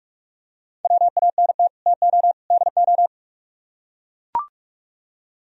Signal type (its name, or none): Morse code